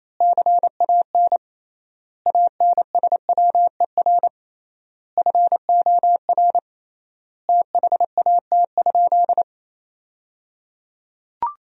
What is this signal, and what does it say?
Morse code, heard as a keyed tone, about 28 wpm, 700 Hz. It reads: CAN ANSWER FOR THAT?
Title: 3x Word-Spacing - Sentences from Top 300 Words - 28wpm